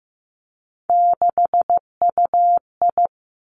Morse code, heard as a keyed tone, about 15 wpm, 700 Hz. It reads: 6UI
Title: Three Challenging Letter-Number Combinations 15wpm